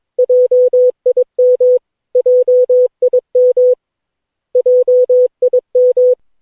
Morse code, heard as a keyed tone, about 22 wpm, 500 Hz. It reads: JIM JIM JIM
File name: cw / Session9 cw training